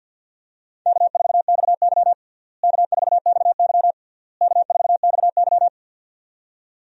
Morse code, heard as a keyed tone, about 51 wpm, 700 Hz. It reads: K4XY K4XY K4XY
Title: USA Call Signs - 4 Characters - Repeated 3x - 50wpm